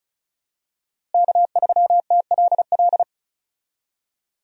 Morse code, heard as a keyed tone, about 34 wpm, 700 Hz. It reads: K3TLL
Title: USA Call Signs - 5 Characters - Repeated 3x - 35wpm